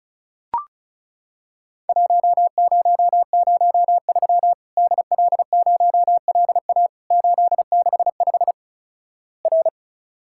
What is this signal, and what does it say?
Morse code, heard as a keyed tone, about 36 wpm, 700 Hz. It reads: 1003 DL0LA 865 R
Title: Worked All Europe DX Contest - QTC Exchange - 35wpm